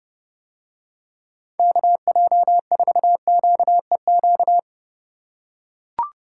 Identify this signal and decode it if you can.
Morse code, heard as a keyed tone, about 29 wpm, 700 Hz. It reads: KJ4QEQ